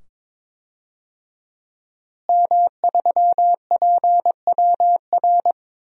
Morse code, heard as a keyed tone, about 22 wpm, 700 Hz. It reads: M3PWR
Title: DXCC-100 Call Signs - 5 Characters - Repeated 2x - 22wpm